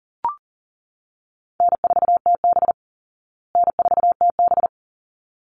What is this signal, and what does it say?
Morse code, heard as a keyed tone, about 40 wpm, 700 Hz. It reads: N4TB N4TB
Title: Easy Call Signs Repeated 2x - 40wpm